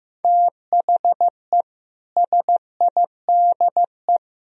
Morse code, heard as a keyed tone, about 15 wpm, 700 Hz. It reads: THE SIDE
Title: Speed-Racing - Sets of 4 Words 15wpm